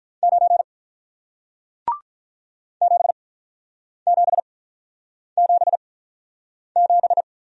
Morse code, heard as a keyed tone, about 48 wpm, spaced as for 9 wpm, 700 Hz. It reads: 9 7777